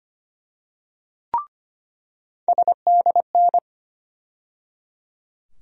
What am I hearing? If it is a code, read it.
Morse code, heard as a keyed tone, about 25 wpm, 700 Hz. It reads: SDN
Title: USA Call Sign Postfixes - 3 Letters 25wpm